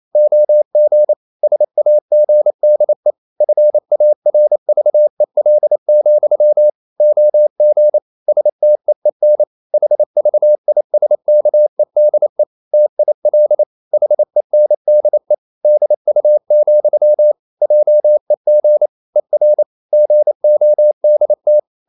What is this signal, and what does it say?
Morse code, heard as a keyed tone, about 28 wpm, 600 Hz. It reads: OG SAGDE FARVEL, OG STEEN HVISKEDE TIL HENDE DU, JEG ER GODT
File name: Ferie QRQ 28wpm